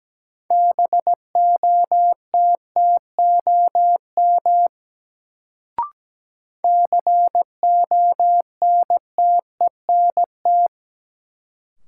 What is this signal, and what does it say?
Morse code, heard as a keyed tone, about 17 wpm, 700 Hz. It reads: BOTTOM CONTENT
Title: Sentences from Top 900 Words - New Word Drill - 17wpm